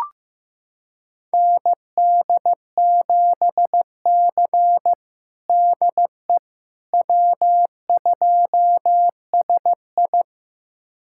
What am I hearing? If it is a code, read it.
Morse code, heard as a keyed tone, about 15 wpm, 700 Hz. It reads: ND7C DE W2SI
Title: USA Callsign DE Callsign - 4 Characters - 15wpm